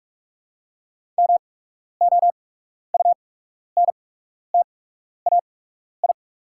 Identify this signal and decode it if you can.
Morse code, heard as a keyed tone, about 48 wpm, spaced as for 12 wpm, 700 Hz. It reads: MOUNTAI
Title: 8x Character Spacing - Top 300 Words - 45wpm